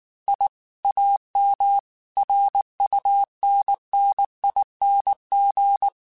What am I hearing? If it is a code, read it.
Morse code, heard as a keyed tone, about 19 wpm, 800 Hz. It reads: I AM RUNNING
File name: QSOs / QSOs-mp3s / kc7sac@18